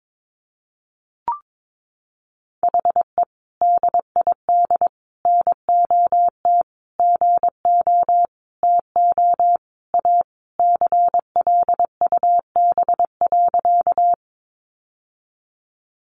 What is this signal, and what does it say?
Morse code, heard as a keyed tone, about 22 wpm, 700 Hz. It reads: HE DID NOT GO TO A CLUB.
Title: Sentences from Top 1200 Words 22wpm